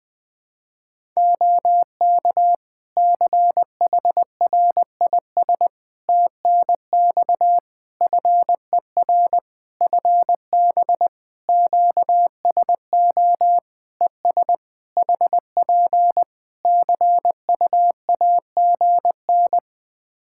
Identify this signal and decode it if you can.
Morse code, heard as a keyed tone, about 21 wpm, 700 Hz. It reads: OK CHRIS TNX FER FB QSO ES HP CUAGN
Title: Practice QSO - Exchange 4 - Ending - 20wpm